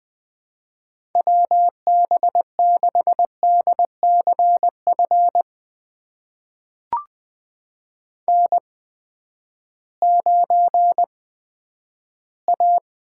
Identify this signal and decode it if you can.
Morse code, heard as a keyed tone, about 20 wpm, spaced as for 11 wpm, 700 Hz. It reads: WB6DCF N9A
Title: USA Call Signs - 8x Character Spacing - 20wpm